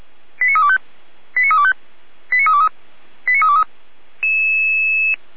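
Five-tone selective call (ZVEI), heard as quick runs of five tones, tones of about 70 ms. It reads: 89325 89325 89323 89323
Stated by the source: ZVEI five-tone call received by SDR, from audio.f85360.feuerwehr-orlamuende.org, a fire brigade (Feuerwehr Orlamünde) server.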